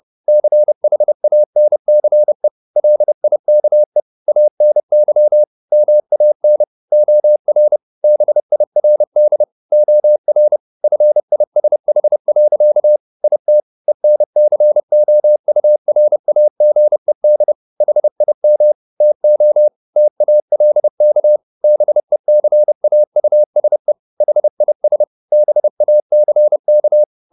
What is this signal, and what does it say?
Morse code, heard as a keyed tone, about 30 wpm, 600 Hz. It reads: CHANCE LIKE ANY MAN OR BIRD OR FISH. IT ENCOURAGED HIM TO TALK BECAUSE HIS BACK
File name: the old man and the sea / the old man and the sea 1 30wpm